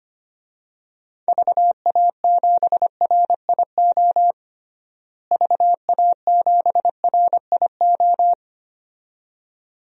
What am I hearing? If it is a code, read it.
Morse code, heard as a keyed tone, about 25 wpm, 700 Hz. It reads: VA7RIO VA7RIO
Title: DXCC-100 Call Signs - 6 Characters - Repeated 2x - 25wpm